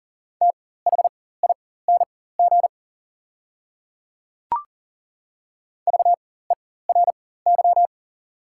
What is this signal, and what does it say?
Morse code, heard as a keyed tone, about 41 wpm, spaced as for 19 wpm, 700 Hz. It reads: THING VERY
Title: 4x Character Spacing - Top 300 Words - 40wpm